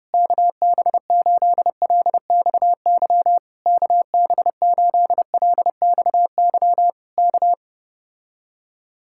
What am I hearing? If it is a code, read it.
Morse code, heard as a keyed tone, about 30 wpm, 700 Hz. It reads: KB8LXY KB8LXY K